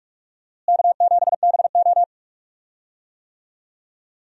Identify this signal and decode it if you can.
Morse code, heard as a keyed tone, about 46 wpm, 700 Hz. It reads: K7BO
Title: USA Call Signs - 4 Characters - Repeated 3x - 45wpm